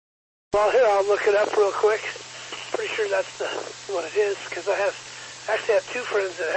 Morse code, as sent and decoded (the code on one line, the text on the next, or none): none